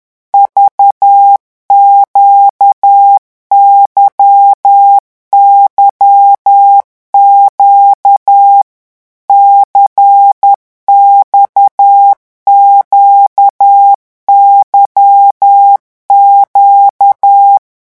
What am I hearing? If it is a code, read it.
Morse code, heard as a keyed tone, about 10 wpm, 800 Hz. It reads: VQYYQ CXQYQ
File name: morse mp3 / revision2-vitesse 10 mots